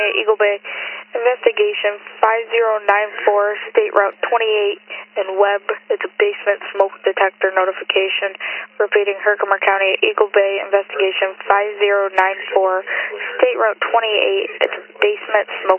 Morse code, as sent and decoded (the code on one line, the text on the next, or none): none